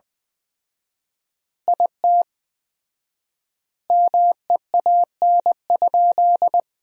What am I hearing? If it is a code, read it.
Morse code, heard as a keyed tone, about 20 wpm, 700 Hz. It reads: IT MEAN?
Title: 4x Word-Spacing - Sentences from Top 200 Words - 20wpm